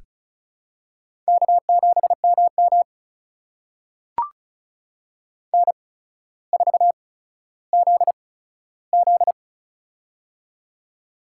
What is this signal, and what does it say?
Morse code, heard as a keyed tone, about 36 wpm, spaced as for 15 wpm, 700 Hz. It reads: K7MM N4ZZ
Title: USA Call Signs - 4 Characters - 8x Character Spacing - 35wpm